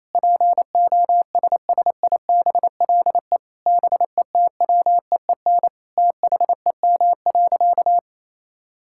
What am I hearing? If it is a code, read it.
Morse code, heard as a keyed tone, about 28 wpm, 700 Hz. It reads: POSSIBLE BETWEEN THEM.
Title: Sentences from Top 1000 Words - Review Part 5 - 28wpm